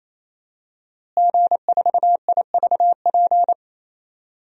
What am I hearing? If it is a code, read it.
Morse code, heard as a keyed tone, about 28 wpm, 700 Hz. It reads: G4IVP
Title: DXCC-100 Call Signs - 5 Characters - Repeated 3x - 28wpm